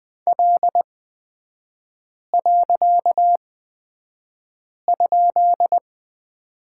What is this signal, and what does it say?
Morse code, heard as a keyed tone, about 20 wpm, spaced as for 5 wpm, 700 Hz. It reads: L.?
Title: 255 Lesson 41e - Period - Three Characters 20wpm